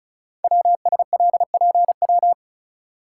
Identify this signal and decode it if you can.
Morse code, heard as a keyed tone, about 36 wpm, 690 Hz. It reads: WSLPW
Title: Code Groups - 5 Letters - 35wpm